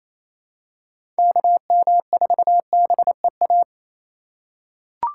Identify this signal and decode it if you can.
Morse code, heard as a keyed tone, about 28 wpm, 700 Hz. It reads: KM4BEA
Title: USA Call Signs - 6 Characters - 3x Character Spacing - 28wpm